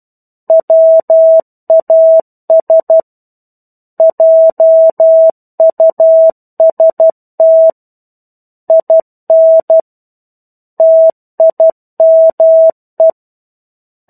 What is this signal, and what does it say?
Morse code, heard as a keyed tone, about 12 wpm, 650 Hz. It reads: WAS JUST IN TIME
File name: 01-Chapter-01 12 WPM